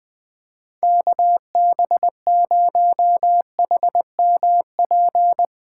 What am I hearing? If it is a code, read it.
Morse code, heard as a keyed tone, about 20 wpm, 700 Hz. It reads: KB0HMP
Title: USA Call Signs - 6 Characters - 8x Character Spacing - 20wpm